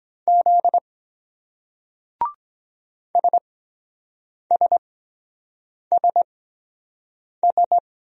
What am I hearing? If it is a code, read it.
Morse code, heard as a keyed tone, about 24 wpm, spaced as for 7 wpm, 700 Hz. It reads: Z SSSS